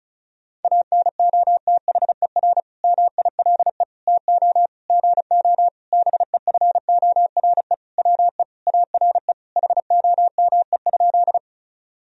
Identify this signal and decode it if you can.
Morse code, heard as a keyed tone, about 34 wpm, 700 Hz. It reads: ANOTHER MILE TO GO BEFORE WE ARE HOME?
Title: Sentences from Top 1000 Words - Review Part 1 - 35wpm